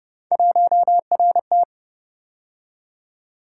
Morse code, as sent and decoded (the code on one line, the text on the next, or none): .---- .-. -
1RT